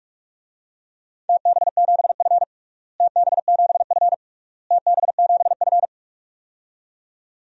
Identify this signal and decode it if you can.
Morse code, heard as a keyed tone, about 43 wpm, 690 Hz. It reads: TB7F TB7F TB7F
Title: DXCC-100 Call Signs - 4 Characters - Repeated 3x - 45wpm